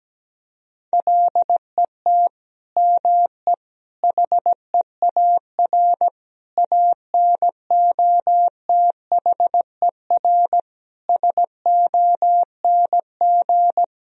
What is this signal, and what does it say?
Morse code, heard as a keyed tone, about 17 wpm, 700 Hz. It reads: LET ME HEAR ANOTHER SONG
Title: Sentences from Top 1000 Words - Review Part 2 - 17wpm